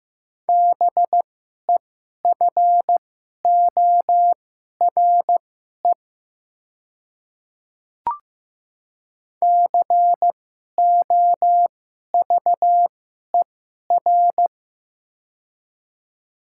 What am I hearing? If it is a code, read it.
Morse code, heard as a keyed tone, about 15 wpm, spaced as for 11 wpm, 700 Hz. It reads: BEFORE COVER
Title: 2x Character Spacing - Top 300 Words - 15wpm